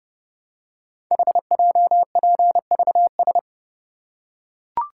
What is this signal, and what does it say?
Morse code, heard as a keyed tone, about 28 wpm, 700 Hz. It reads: HJPVS
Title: Code Groups - 5 Letters - 30wpm